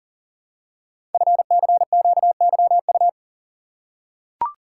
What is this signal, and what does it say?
Morse code, heard as a keyed tone, about 40 wpm, 700 Hz. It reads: FCQYU
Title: Code Groups - 5 Letters - 40wpm